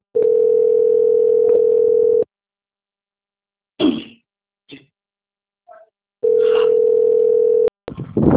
Ringback tone (North America), 440 + 480 Hz.
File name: out- - - - - .20707